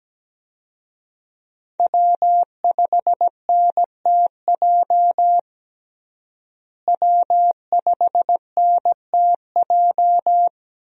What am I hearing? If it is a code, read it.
Morse code, heard as a keyed tone, about 17 wpm, 700 Hz. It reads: W5NTJ W5NTJ
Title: USA Call Signs - 5 Characters - Repeated 3x - 17wpm